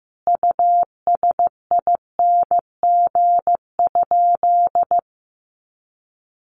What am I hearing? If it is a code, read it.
Morse code, heard as a keyed tone, about 14 wpm, 700 Hz. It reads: USING?
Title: Common QSO Phrases 15wpm